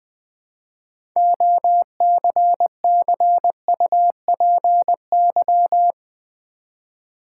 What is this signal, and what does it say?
Morse code, heard as a keyed tone, about 21 wpm, 700 Hz. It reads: OCCUPY
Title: 6 Letter Words - First 3 Letters Given - 20wpm